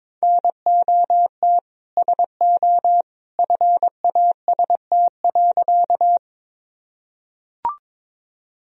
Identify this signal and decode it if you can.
Morse code, heard as a keyed tone, about 22 wpm, 700 Hz. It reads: NOT SO FAST.
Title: Sentences from Top 1000 Words - Review Part 2 - 22wpm